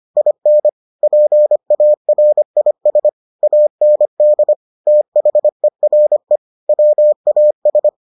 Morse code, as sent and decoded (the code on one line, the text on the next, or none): .. -. / .--. .- .-. .. ... / .- -. -.. / - .... .-. / .-- .- ...
IN PARIS AND THERE WAS